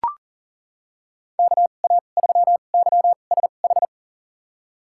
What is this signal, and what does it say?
Morse code, heard as a keyed tone, about 37 wpm, spaced as for 28 wpm, 700 Hz. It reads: KA3YSH
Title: USA Call Signs - 6 Characters - 2x Character Spacing - 40wpm